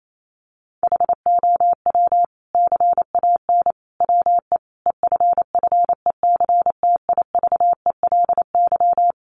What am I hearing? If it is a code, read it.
Morse code, heard as a keyed tone, about 28 wpm, 700 Hz. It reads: HOW CAN WE EFFECTIVELY
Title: Sentences from Top 1400 Words 28wpm